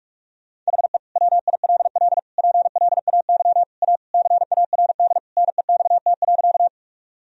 Morse code, heard as a keyed tone, about 46 wpm, 700 Hz. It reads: HE WILL PLAY A CARD NEXT.
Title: Sentences from Top 1000 Words - Review Part 4 - 45wpm